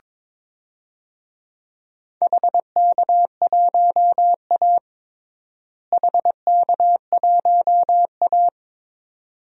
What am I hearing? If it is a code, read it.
Morse code, heard as a keyed tone, about 21 wpm, 700 Hz. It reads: HK1A HK1A
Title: DXCC-100 Call Signs - 4 Characters - Repeated 2x - 22wpm